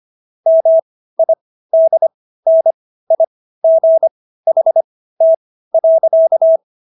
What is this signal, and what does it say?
Morse code, heard as a keyed tone, about 25 wpm, spaced as for 15 wpm, 650 Hz. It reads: MIDNIGHT.